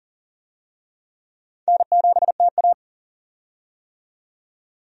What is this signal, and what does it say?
Morse code, heard as a keyed tone, about 38 wpm, 700 Hz. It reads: N7TA
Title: USA Call Signs - 4 Characters - 40wpm